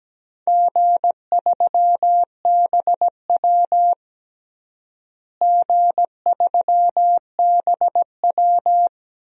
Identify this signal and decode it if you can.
Morse code, heard as a keyed tone, about 17 wpm, 700 Hz. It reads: G3BW G3BW
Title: DXCC-100 Call Signs - 4 Characters - Repeated 3x - 17wpm